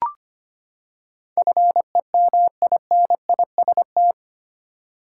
Morse code, heard as a keyed tone, about 25 wpm, 700 Hz. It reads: FEMINIST
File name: Eight Letter Words - 25wpm